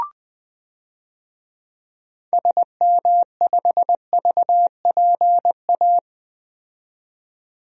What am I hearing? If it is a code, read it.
Morse code, heard as a keyed tone, about 20 wpm, 700 Hz. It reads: SM5VPA
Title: DXCC-100 Call Signs - 6 Characters - 20wpm